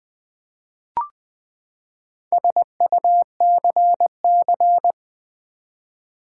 Morse code, heard as a keyed tone, about 20 wpm, 700 Hz. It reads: SUCC
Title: Build A Word - 4 Letters At A Time - 10-Letter Words - 20wpm